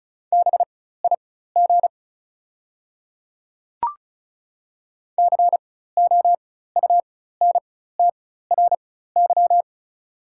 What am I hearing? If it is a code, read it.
Morse code, heard as a keyed tone, about 36 wpm, spaced as for 16 wpm, 700 Hz. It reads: BIG COUNTRY
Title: 4x Character Spacing - Top 500 Words - 35wpm